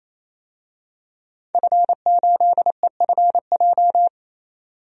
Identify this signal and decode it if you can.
Morse code, heard as a keyed tone, about 28 wpm, 700 Hz. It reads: F8EFJ
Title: DXCC-100 Call Signs - 5 Characters - 28wpm